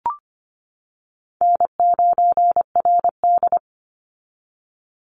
Morse code, heard as a keyed tone, about 25 wpm, 700 Hz. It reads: N9RD
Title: Easy Call Signs - 25wpm